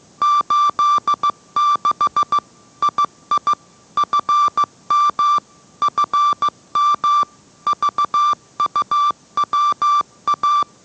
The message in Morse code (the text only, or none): 86 II FM FM VUWA